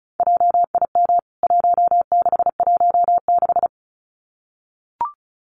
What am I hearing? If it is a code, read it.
Morse code, heard as a keyed tone, about 34 wpm, 700 Hz. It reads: JIM 1616
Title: CWT Exchange 35wpm